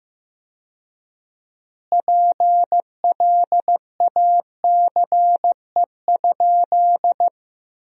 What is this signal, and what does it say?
Morse code, heard as a keyed tone, about 15 wpm, 700 Hz. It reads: PLACE?
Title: 4x Word-Spacing - Sentences from Top 200 Words - 15wpm